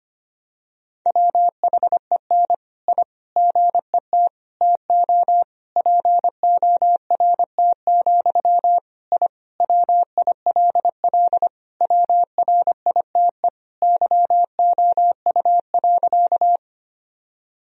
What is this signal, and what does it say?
Morse code, heard as a keyed tone, about 25 wpm, 700 Hz. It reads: WHEN I GET TO PORT, I WILL WRITE YOU.
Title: Sentences from Top 1000 Words - Review Part 1 - 25wpm